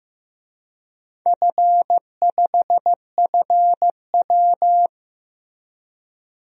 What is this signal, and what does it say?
Morse code, heard as a keyed tone, about 15 wpm, 700 Hz. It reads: F5FW
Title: DXCC-100 Call Signs - 4 Characters - Repeated 3x - 15wpm